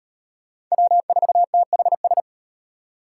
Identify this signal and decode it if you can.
Morse code, heard as a keyed tone, about 36 wpm, 700 Hz. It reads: W4THS